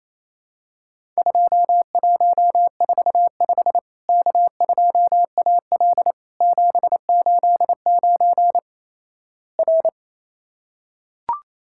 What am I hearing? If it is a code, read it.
Morse code, heard as a keyed tone, about 28 wpm, 700 Hz. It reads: 2145 K2AL 789 R
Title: Worked All Europe DX Contest - QTC Exchange - 28wpm